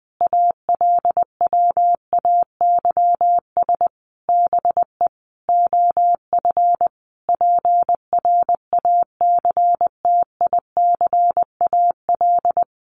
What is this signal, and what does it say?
Morse code, heard as a keyed tone, about 20 wpm, 700 Hz. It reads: ALWAYS BE OF PRACTICAL
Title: Sentences from Top 1400 Words 20wpm